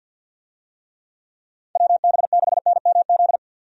Morse code, heard as a keyed tone, about 52 wpm, 700 Hz. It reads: WB6NMZ
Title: USA Call Signs - 6 Characters - 50wpm